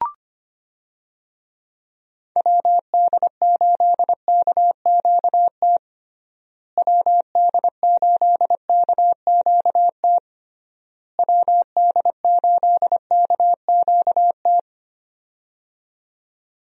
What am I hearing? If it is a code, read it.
Morse code, heard as a keyed tone, about 25 wpm, 700 Hz. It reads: WD8KQT WD8KQT WD8KQT
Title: USA Call Signs - 6 Characters - Repeated 3x - 25wpm